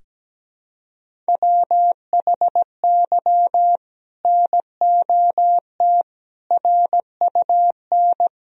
Morse code, heard as a keyed tone, about 17 wpm, 700 Hz. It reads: WHY NOT RUN
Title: Sentences from Top 1000 Words - Review Part 1 - 17wpm